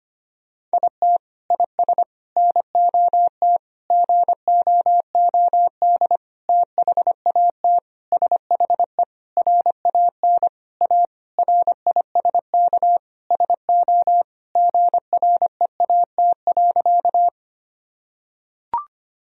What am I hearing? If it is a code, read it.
Morse code, heard as a keyed tone, about 25 wpm, 700 Hz. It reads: IT IS NOT GOOD THAT SHE RAN A RISK SO GREAT.
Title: Sentences from Top 1000 Words - Review Part 4 - 25wpm